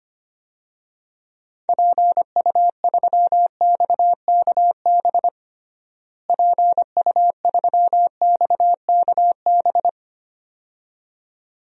Morse code, heard as a keyed tone, about 25 wpm, 700 Hz. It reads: PU3XKB PU3XKB
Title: DXCC-100 Call Signs - 6 Characters - Repeated 2x - 25wpm